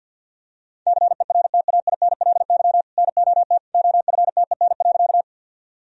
Morse code, heard as a keyed tone, about 51 wpm, 700 Hz. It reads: CERTAINLY NOT OFTEN.
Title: Sentences from Top 1000 Words - Review Part 1 - 50wpm